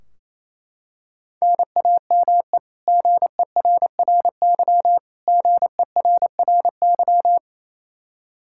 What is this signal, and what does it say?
Morse code, heard as a keyed tone, about 28 wpm, 700 Hz. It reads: NAME GERRY GERRY
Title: Practice QSO - Exchange 1 - Beginning - 28wpm